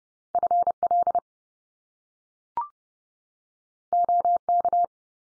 Morse code, heard as a keyed tone, about 30 wpm, 700 Hz. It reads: FL OK
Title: US States 30WPM